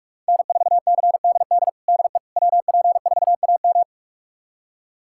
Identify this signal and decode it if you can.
Morse code, heard as a keyed tone, about 46 wpm, 700 Hz. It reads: N4CDD DE WP4AM
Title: USA Callsign DE Callsign - 5 Characters - 45wpm